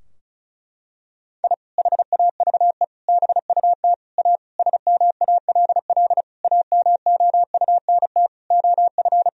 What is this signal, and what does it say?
Morse code, heard as a keyed tone, about 36 wpm, 700 Hz. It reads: I HAVE BUT A SMALL AMOUNT OF
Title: Sentences from Top 1000 Words - Review Part 4 - 35wpm